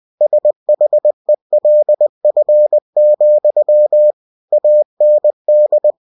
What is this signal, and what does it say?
Morse code, heard as a keyed tone, about 20 wpm, 600 Hz. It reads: SHELF, AND
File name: cw-books / winnie-the-pooh / winnie-20wpm-page-0039